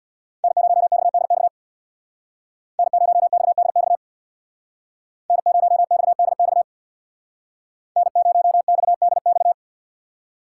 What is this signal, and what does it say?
Morse code, heard as a keyed tone, about 63 wpm, 700 Hz. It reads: N0XDX N0XDX N0XDX N0XDX